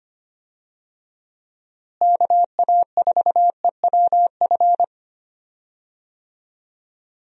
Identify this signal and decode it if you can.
Morse code, heard as a keyed tone, about 25 wpm, 700 Hz. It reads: KA4EWF
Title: USA Call Signs - 6 Characters - Repeated 3x - 25wpm